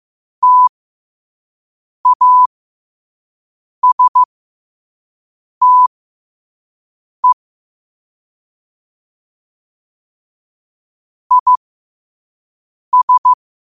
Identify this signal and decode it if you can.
Morse code, heard as a keyed tone, about 15 wpm, spaced as for 5 wpm, 1000 Hz. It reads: TASTE IS
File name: qotdfiles / 5wpm